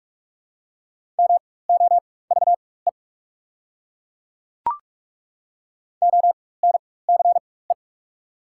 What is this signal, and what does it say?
Morse code, heard as a keyed tone, about 42 wpm, spaced as for 21 wpm, 690 Hz. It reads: MOVE ONCE
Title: 4x Character Spacing - Top 500 Words - 45wpm